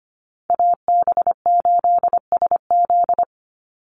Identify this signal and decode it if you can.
Morse code, heard as a keyed tone, about 25 wpm, 700 Hz. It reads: AB8SZ
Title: USA Call Signs 25wpm